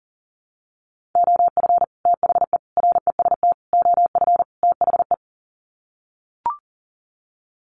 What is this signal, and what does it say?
Morse code, heard as a keyed tone, about 40 wpm, 700 Hz. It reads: OF THE REST OF THE
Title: Sets of 5 Words 40wpm